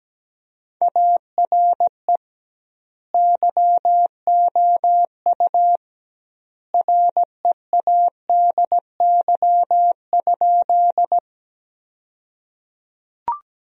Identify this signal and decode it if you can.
Morse code, heard as a keyed tone, about 17 wpm, 700 Hz. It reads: ARE YOU READY?
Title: 2x Word-Spacing - Sentences from Top 400 Words - 17wpm